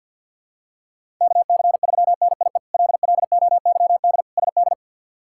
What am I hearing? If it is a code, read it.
Morse code, heard as a keyed tone, about 52 wpm, 690 Hz. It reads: KC3NIE LLOYD SD
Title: K1USN SST QSO Practice - 50wpm